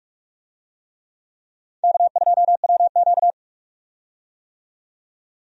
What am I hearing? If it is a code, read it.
Morse code, heard as a keyed tone, about 46 wpm, 700 Hz. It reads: K2WQ
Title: USA Call Signs - 4 Characters - 45wpm